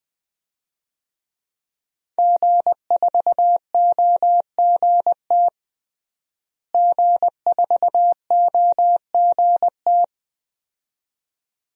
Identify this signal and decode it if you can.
Morse code, heard as a keyed tone, about 21 wpm, 700 Hz. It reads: G4OGT G4OGT
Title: DXCC-100 Call Signs - 5 Characters - Repeated 2x - 20wpm